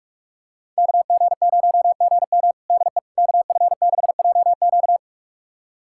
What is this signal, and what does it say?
Morse code, heard as a keyed tone, about 46 wpm, 700 Hz. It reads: KG0GM DE KF6JQ
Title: USA Callsign DE Callsign - 5 Characters - 45wpm